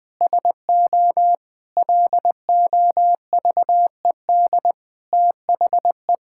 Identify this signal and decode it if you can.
Morse code, heard as a keyed tone, about 21 wpm, 700 Hz. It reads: SO LOVED THE
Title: Sets of 8 Words - 20wpm